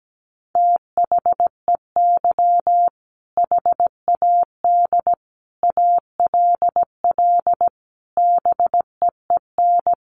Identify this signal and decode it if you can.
Morse code, heard as a keyed tone, about 17 wpm, 700 Hz. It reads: THEY HAD ALL BEEN
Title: Sentences from Top 200 Words 17wpm